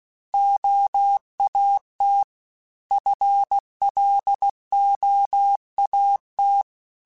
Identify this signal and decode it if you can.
Morse code, heard as a keyed tone, about 16 wpm, 800 Hz. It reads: OAT FLOAT
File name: FL / FL mp3s / Fast FL @15